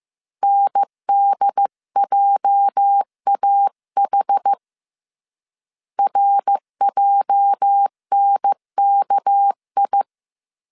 Morse code, heard as a keyed tone, about 15 wpm, 800 Hz. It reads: NDJAH RJNKI
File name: CW / DET / 1 9 1500C S C 14wpm 3245